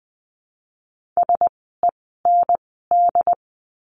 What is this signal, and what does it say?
Morse code, heard as a keyed tone, about 20 wpm, spaced as for 14 wpm, 700 Hz. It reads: SEND
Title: 068 Lesson 12f - Letter d - Words 20wpm